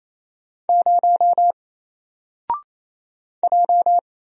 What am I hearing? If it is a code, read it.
Morse code, heard as a keyed tone, about 27 wpm, 700 Hz. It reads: 0 J
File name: Single-Letter-Number-28wpm